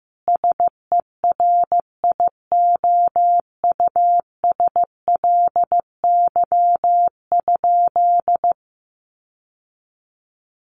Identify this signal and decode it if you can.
Morse code, heard as a keyed tone, about 15 wpm, 700 Hz. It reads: SERIOUSLY?
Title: Sentences from Top 1200 Words 15wpm